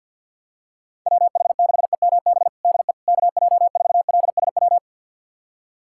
Morse code, heard as a keyed tone, about 53 wpm, 700 Hz. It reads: WH6EMB DE KJ4LSW
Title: USA Callsign DE Callsign 50wpm